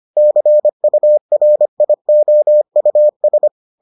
Morse code, heard as a keyed tone, about 25 wpm, 600 Hz. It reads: CURIOUS